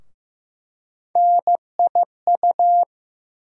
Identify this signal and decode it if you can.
Morse code, heard as a keyed tone, about 15 wpm, 700 Hz. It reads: NIU
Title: USA Call Sign Postfixes - 3 Letters 15wpm